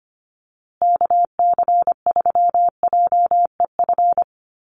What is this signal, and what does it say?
Morse code, heard as a keyed tone, about 25 wpm, 700 Hz. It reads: KC3JEF
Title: USA Call Signs Repeated 3x 25wpm